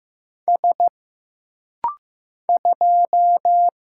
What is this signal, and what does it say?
Morse code, heard as a keyed tone, about 15 wpm, 700 Hz. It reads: S 2
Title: Single Letter-Number Alt - Rapid-Fire 15wpm